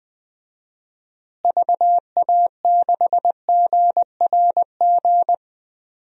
Morse code, heard as a keyed tone, about 20 wpm, 700 Hz. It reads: VA6GRG
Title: DXCC-100 Call Signs - 6 Characters - Repeated 3x - 20wpm